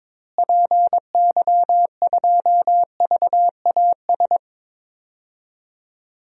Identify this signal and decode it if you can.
Morse code, heard as a keyed tone, about 22 wpm, 700 Hz. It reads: PY2VAS
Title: DXCC-100 Call Signs - 6 Characters - Repeated 2x - 22wpm